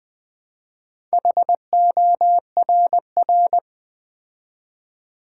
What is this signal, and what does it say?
Morse code, heard as a keyed tone, about 20 wpm, 700 Hz. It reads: HORR
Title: Build A Word - 2 Letters At A Time - 6-Letter Words - 20wpm